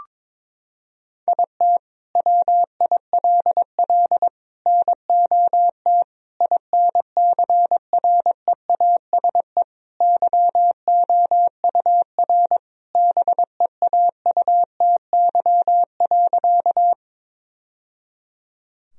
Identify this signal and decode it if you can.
Morse code, heard as a keyed tone, about 21 wpm, 700 Hz. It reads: IT WILL NOT INCREASE YOUR BEAUTY.
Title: Sentences from Top 1000 Words - Review Part 4 - 22wpm